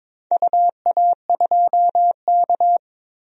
Morse code, heard as a keyed tone, about 22 wpm, 700 Hz. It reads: UA2K
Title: DXCC-100 Call Signs - 4 Characters - 22wpm